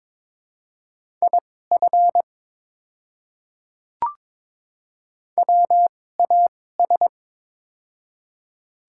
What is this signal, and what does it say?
Morse code, heard as a keyed tone, about 22 wpm, spaced as for 16 wpm, 700 Hz. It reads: IF WAS
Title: 2x Character Spacing - Top 200 Words - 22wpm